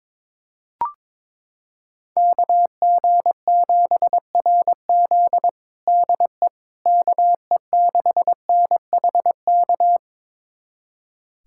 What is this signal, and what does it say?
Morse code, heard as a keyed tone, about 22 wpm, 700 Hz. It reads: KG7RZ DE KE6NHK
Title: USA Callsign DE Callsign 22wpm